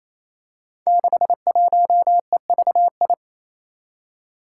Morse code, heard as a keyed tone, about 28 wpm, 700 Hz. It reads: 61EVI
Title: Five Challenging Letter-Number Combinations 28wpm